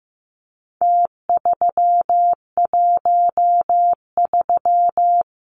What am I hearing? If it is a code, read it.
Morse code, heard as a keyed tone, about 15 wpm, 700 Hz. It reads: T313
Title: Numbers - Contest Style - 4-Digit 15wpm